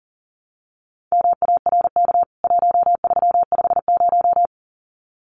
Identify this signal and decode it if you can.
Morse code, heard as a keyed tone, about 40 wpm, 700 Hz. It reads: MARK 1350
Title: CWT Exchange 40wpm